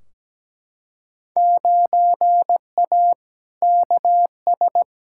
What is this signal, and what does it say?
Morse code, heard as a keyed tone, about 17 wpm, 700 Hz. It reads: 9A KS
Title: Field Day Exchanges Encore - 17wpm